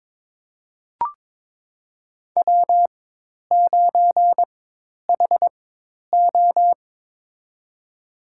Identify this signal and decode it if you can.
Morse code, heard as a keyed tone, about 22 wpm, spaced as for 10 wpm, 700 Hz. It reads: W9HO
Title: USA Call Signs - 4 Characters - 4x Character Spacing - 22wpm